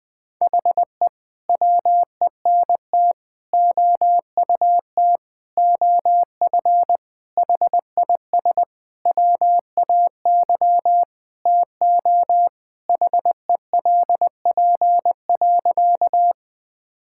Morse code, heard as a keyed tone, about 20 wpm, 700 Hz. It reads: HE WENT OUT OF HIS WAY TO HELP.
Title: Sentences with 6-Word Phrases - 20wpm